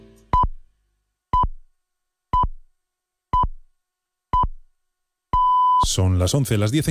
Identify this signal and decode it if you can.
Radio time signal (time pips), with speech. It is heard from 0.3 to 5.8 s.